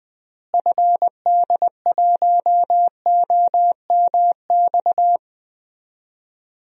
Morse code, heard as a keyed tone, about 21 wpm, 700 Hz. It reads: FD1OMX
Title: DXCC-100 Call Signs - 6 Characters - Repeated 2x - 20wpm